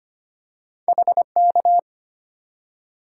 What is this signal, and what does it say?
Morse code, heard as a keyed tone, about 25 wpm, 700 Hz. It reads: HK
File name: Two Letter-Number Combinations - 25wpm